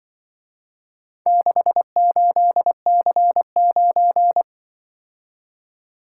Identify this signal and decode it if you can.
Morse code, heard as a keyed tone, about 25 wpm, 700 Hz. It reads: 68C9